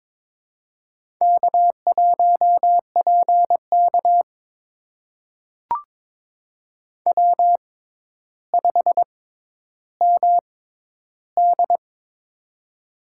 Morse code, heard as a keyed tone, about 22 wpm, spaced as for 11 wpm, 700 Hz. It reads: K1PK W5MD
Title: USA Call Signs - 4 Characters - 6x Character Spacing - 22wpm